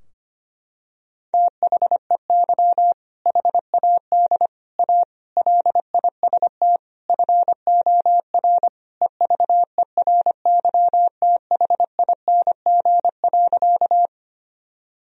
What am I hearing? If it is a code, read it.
Morse code, heard as a keyed tone, about 25 wpm, 700 Hz. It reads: THEY HAD A LIST FOR EVERYTHING.
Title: Sentences from Top 1000 Words - Review Part 2 - 25wpm